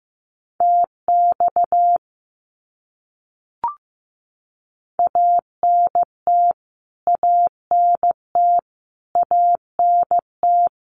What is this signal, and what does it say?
Morse code, heard as a keyed tone, about 15 wpm, 700 Hz. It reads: TX ANT ANT ANT
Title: Abbreviations Repeated 3x - 15wpm